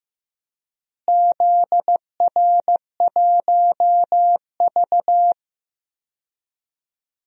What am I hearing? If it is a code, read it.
Morse code, heard as a keyed tone, about 16 wpm, 700 Hz. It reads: ZR1V
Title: DXCC-100 Call Signs - 4 Characters - Repeated 3x - 15wpm